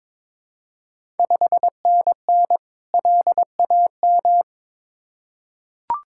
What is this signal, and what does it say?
Morse code, heard as a keyed tone, about 22 wpm, 700 Hz. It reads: 5NN LAM
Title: Ontario QSO Party Exchange 22wpm